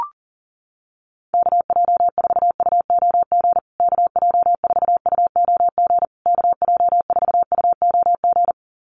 Morse code, heard as a keyed tone, about 38 wpm, 700 Hz. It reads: KJ4UOG KJ4UOG KJ4UOG
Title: USA Call Signs Repeated 3x 40wpm